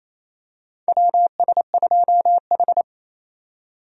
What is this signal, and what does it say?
Morse code, heard as a keyed tone, about 28 wpm, 700 Hz. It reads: WS2H
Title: USA Call Signs - 4 Characters - 28wpm